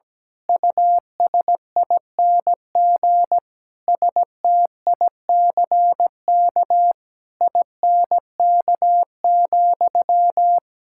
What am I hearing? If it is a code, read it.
Morse code, heard as a keyed tone, about 17 wpm, 700 Hz. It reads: USING STICK INK,
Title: Pangram Challenge v2 - 17wpm